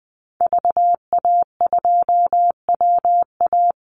Morse code, heard as a keyed tone, about 20 wpm, 700 Hz. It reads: VA2WA
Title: Call Signs Repeated 2x - 20wpm